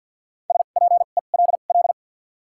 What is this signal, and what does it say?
Morse code, heard as a keyed tone, about 52 wpm, spaced as for 37 wpm, 700 Hz. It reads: SPELL
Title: 2x Character Spacing - Top 300 Words - 50wpm